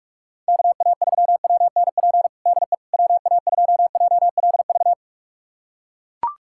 Morse code, heard as a keyed tone, about 41 wpm, 700 Hz. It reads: KA3WNP DE WA2JLV
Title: USA Callsign DE Callsign - 6 Characters - 45wpm